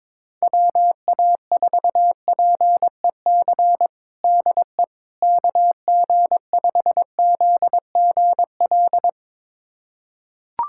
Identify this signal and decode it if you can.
Morse code, heard as a keyed tone, about 21 wpm, 700 Hz. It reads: WA4PEC DE KG5ZGL
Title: USA Callsign DE Callsign - 6 Characters - 22wpm